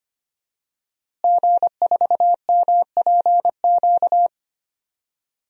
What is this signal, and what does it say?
Morse code, heard as a keyed tone, about 25 wpm, 700 Hz. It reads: G4MPQ
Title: DXCC-100 Call Signs - 5 Characters - Repeated 3x - 25wpm